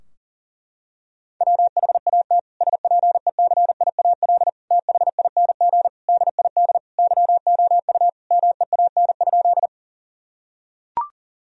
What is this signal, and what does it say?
Morse code, heard as a keyed tone, about 40 wpm, 700 Hz. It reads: WHAT SPECIAL THING DID YOU MEAN?
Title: Sentences from Top 1000 Words - Review Part 3 - 40wpm